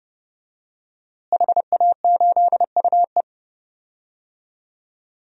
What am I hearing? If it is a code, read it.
Morse code, heard as a keyed tone, about 30 wpm, 700 Hz. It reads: HA8UE
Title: DXCC-100 Call Signs - 5 Characters - Repeated 2x - 30wpm